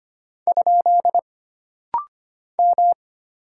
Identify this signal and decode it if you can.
Morse code, heard as a keyed tone, about 25 wpm, 700 Hz. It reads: ? M
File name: Single Character (Letter:Number:Punctuation:Prosign) - Rapid-Fire - 25wpm